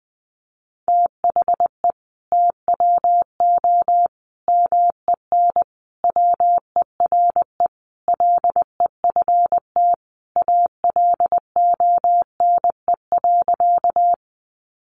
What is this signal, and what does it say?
Morse code, heard as a keyed tone, about 20 wpm, 700 Hz. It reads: THE TWO MEN WERE LEFT ALONE.
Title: Sentences from Top 1000 Words 20wpm